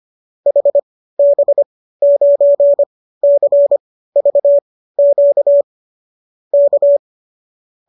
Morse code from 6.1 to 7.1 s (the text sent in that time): K